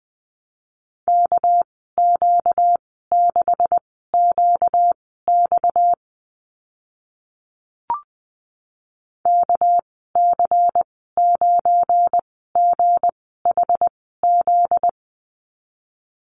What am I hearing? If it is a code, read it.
Morse code, heard as a keyed tone, about 20 wpm, spaced as for 14 wpm, 700 Hz. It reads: KQ6QX KC9GHZ
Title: 245 Lesson 39g - Letter z - Call Signs 20wpm